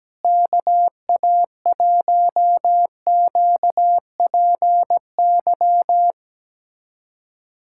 Morse code, heard as a keyed tone, about 17 wpm, 700 Hz. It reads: KA1QPY
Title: USA Callsign DE Callsign 17wpm